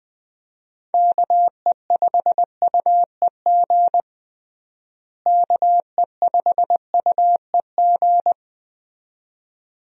Morse code, heard as a keyed tone, about 21 wpm, 700 Hz. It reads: KE5UEG KE5UEG
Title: USA Call Signs - 6 Characters - Repeated 2x - 20wpm